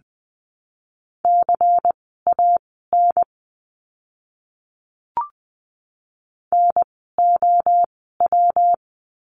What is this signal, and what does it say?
Morse code, heard as a keyed tone, about 20 wpm, spaced as for 14 wpm, 700 Hz. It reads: CAN NOW
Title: 185 Lesson 30f - Letter b - Words 20wpm